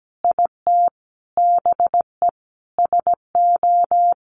Morse code, heard as a keyed tone, about 17 wpm, 700 Hz. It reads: IT BE SO
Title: Sentences from Top 1300 Words 17wpm